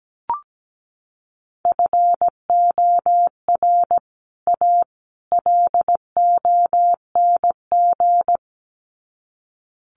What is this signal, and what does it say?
Morse code, heard as a keyed tone, about 17 wpm, 700 Hz. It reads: FOR A LONG
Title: Sets of 3 Words 17wpm